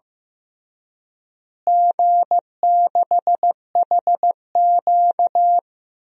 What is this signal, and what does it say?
Morse code, heard as a keyed tone, about 15 wpm, 700 Hz. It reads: G6HQ
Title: DXCC-100 Call Signs - 4 Characters - Repeated 3x - 15wpm